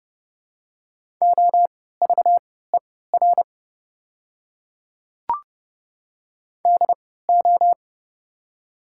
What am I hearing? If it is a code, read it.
Morse code, heard as a keyed tone, about 30 wpm, spaced as for 17 wpm, 700 Hz. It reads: OVER DO
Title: 3x Character Spacing - Top 300 Words - 30wpm